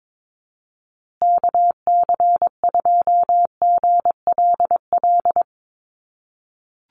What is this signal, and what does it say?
Morse code, heard as a keyed tone, about 22 wpm, 700 Hz. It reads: KC2GLL
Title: USA Call Signs Repeated 3x 22wpm